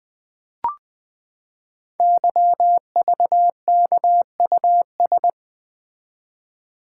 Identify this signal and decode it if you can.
Morse code, heard as a keyed tone, about 20 wpm, 700 Hz. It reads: YVKUS
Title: Code Groups - 5 Letters - 20wpm